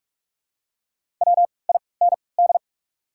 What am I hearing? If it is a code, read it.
Morse code, heard as a keyed tone, about 47 wpm, spaced as for 26 wpm, 690 Hz. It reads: WIND
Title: 3x Character Spacing - Top 500 Words - 45wpm